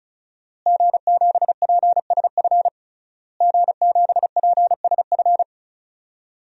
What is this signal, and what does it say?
Morse code, heard as a keyed tone, about 34 wpm, 700 Hz. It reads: G7PSF G7PSF
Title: DXCC-100 Call Signs - 5 Characters - Repeated 3x - 35wpm